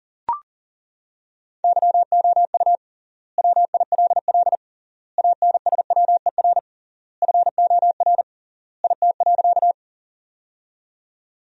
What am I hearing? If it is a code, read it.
Morse code, heard as a keyed tone, about 40 wpm, 700 Hz. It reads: YOU WILL ANSWER FOR IT.
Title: 3x Word-Spacing - Sentences from Top 300 Words - 40wpm